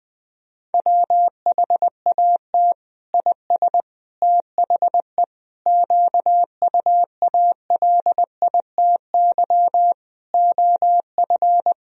Morse code, heard as a keyed tone, about 20 wpm, 700 Hz. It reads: WHAT IS THE QUALITY OF
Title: Sentences from Top 1000 Words - Review Part 4 - 20wpm